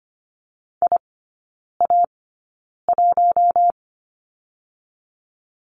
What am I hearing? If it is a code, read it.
Morse code, heard as a keyed tone, about 25 wpm, spaced as for 9 wpm, 700 Hz. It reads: IA1
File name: Lesson 14e - Number 2 - Three Characters 25wpm